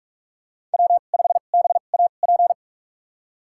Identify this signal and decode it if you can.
Morse code, heard as a keyed tone, about 46 wpm, spaced as for 32 wpm, 700 Hz. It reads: W5BAP
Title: USA Call Signs - 5 Characters - 2x Character Spacing - 45wpm